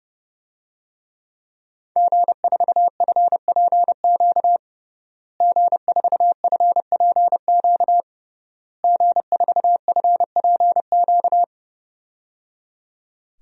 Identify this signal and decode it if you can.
Morse code, heard as a keyed tone, about 30 wpm, 700 Hz. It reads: G4FPQ G4FPQ G4FPQ
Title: DXCC-100 Call Signs - 5 Characters - Repeated 3x - 30wpm